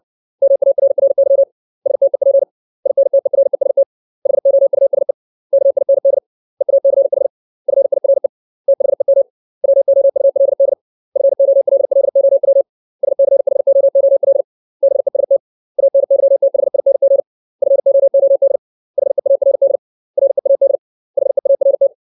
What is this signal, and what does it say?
Morse code, heard as a keyed tone, about 60 wpm, 550 Hz. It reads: MARRY STEP INTEREST HOUSE KIND EACH FIRE THEM WOULD FOLLOW SCHOOL BEST ANYTHING FOOD HEARD READ HEART
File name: 632 words / 632 Words 60WPM